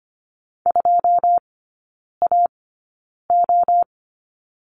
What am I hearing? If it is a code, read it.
Morse code, heard as a keyed tone, about 25 wpm, spaced as for 9 wpm, 700 Hz. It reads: 2AO
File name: course / 25:10wpm / 093 Lesson 16e - Letter c - Three Characters 25wpm